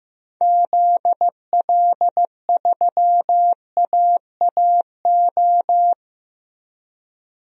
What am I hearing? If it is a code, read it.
Morse code, heard as a keyed tone, about 15 wpm, 700 Hz. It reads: ZL3AAO
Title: DXCC-100 Call Signs - 6 Characters - Repeated 2x - 15wpm